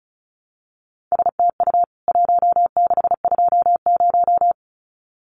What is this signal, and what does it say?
Morse code, heard as a keyed tone, about 34 wpm, 700 Hz. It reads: STU 1620
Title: CWT Exchange 35wpm